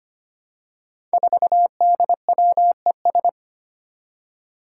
Morse code, heard as a keyed tone, about 25 wpm, 700 Hz. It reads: 4DWES
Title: Five Challenging Letter-Number Combinations 25wpm